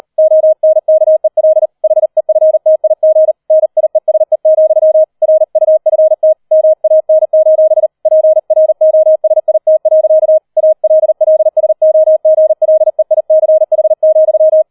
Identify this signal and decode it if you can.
Morse code, heard as a keyed tone, about 38 wpm, 630 Hz. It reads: ONKEL HEFTIG NIESE, RUFT MAN: PROSIT. ALLSOGLEICH,
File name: 40wpm